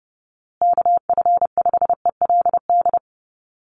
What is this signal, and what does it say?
Morse code, heard as a keyed tone, about 30 wpm, 700 Hz. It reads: KF5ELD
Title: USA Call Signs Repeated 2x 30wpm